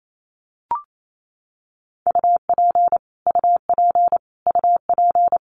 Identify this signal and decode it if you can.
Morse code, heard as a keyed tone, about 28 wpm, 700 Hz. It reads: UP UP UP
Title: Top 500 Words Repeated 28wpm